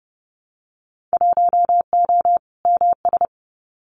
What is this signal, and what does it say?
Morse code, heard as a keyed tone, about 30 wpm, 700 Hz. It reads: 1O MS
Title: Winter Field Day Exchanges 30wpm